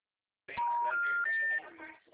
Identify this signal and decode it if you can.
Special information tone, with speech.
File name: out- -2003